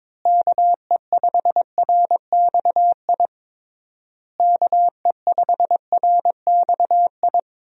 Morse code, heard as a keyed tone, about 22 wpm, 700 Hz. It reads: KE5RXI KE5RXI
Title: USA Call Signs - 6 Characters - Repeated 3x - 22wpm